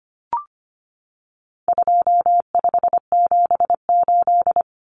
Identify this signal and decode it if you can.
Morse code, heard as a keyed tone, about 25 wpm, 700 Hz. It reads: 2578
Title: Common Cut Numbers - 4-Digit 25wpm